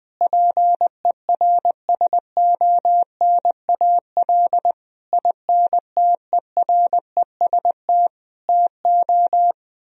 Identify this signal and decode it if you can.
Morse code, heard as a keyed tone, about 20 wpm, 700 Hz. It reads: PERSONAL INTEREST TO
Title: Sentences from Top 1000 Words - Review Part 4 - 20wpm